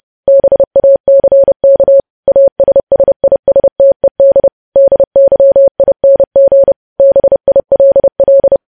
Morse code, heard as a keyed tone, about 30 wpm, 550 Hz. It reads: BACK ASSISTED DYING BILL
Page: news headlines in morse code at 30 wpm